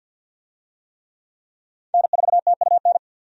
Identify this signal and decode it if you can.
Morse code, heard as a keyed tone, about 52 wpm, 700 Hz. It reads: N4TUN
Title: USA Call Signs - 5 Characters - 50wpm